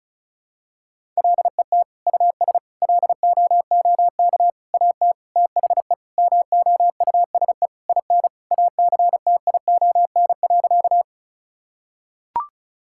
Morse code, heard as a keyed tone, about 36 wpm, 700 Hz. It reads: LET US LOOK AT THE MOUSE IN ACTION.
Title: Sentences from Top 1000 Words - Review Part 5 - 35wpm